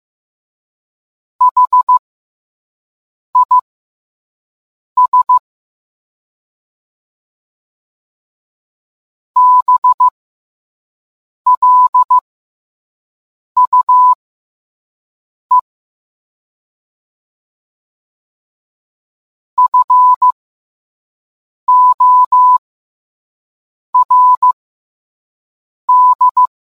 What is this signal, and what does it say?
Morse code, heard as a keyed tone, about 15 wpm, spaced as for 5 wpm, 1000 Hz. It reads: HIS BLUE FORD